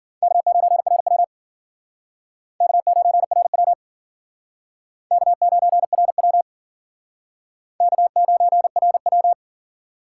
Morse code, heard as a keyed tone, about 50 wpm, 700 Hz. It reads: K9RW K9RW K9RW K9RW
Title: Speed Racing - USA 1x2 Call Signs 40wpm